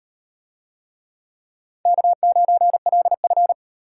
Morse code, heard as a keyed tone, about 38 wpm, 700 Hz. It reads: K9LF